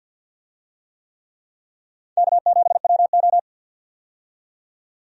Morse code, heard as a keyed tone, about 47 wpm, 700 Hz. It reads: K7WO
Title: USA Call Signs - 4 Characters - 50wpm